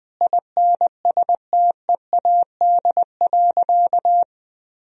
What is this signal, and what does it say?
Morse code, heard as a keyed tone, about 20 wpm, 700 Hz. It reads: INSTEAD.
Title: Sentences from Top 1000 Words - Review Part 4 - 20wpm